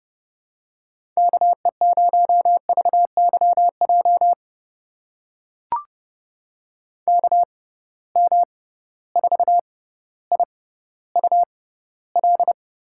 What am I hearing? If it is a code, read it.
Morse code, heard as a keyed tone, about 30 wpm, spaced as for 15 wpm, 700 Hz. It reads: KE0VYJ KM4IUL
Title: USA Call Signs - 6 Characters - 6x Character Spacing - 30wpm